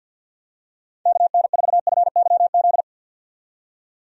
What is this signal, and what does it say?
Morse code, heard as a keyed tone, about 48 wpm, 700 Hz. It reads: KN4FYZ
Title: USA Call Signs - 6 Characters - 50wpm